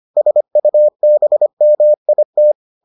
Morse code, heard as a keyed tone, about 25 wpm, 600 Hz. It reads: SUBMIT